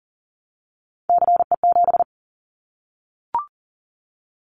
Morse code, heard as a keyed tone, about 39 wpm, 700 Hz. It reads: CE7
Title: Callsign Prefixes for Top 100 DXCC - 40wpm